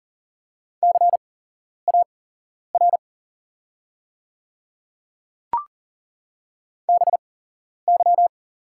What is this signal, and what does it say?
Morse code, heard as a keyed tone, about 40 wpm, spaced as for 11 wpm, 690 Hz. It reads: CAR BY
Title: 8x Character Spacing - Top 500 Words - 40wpm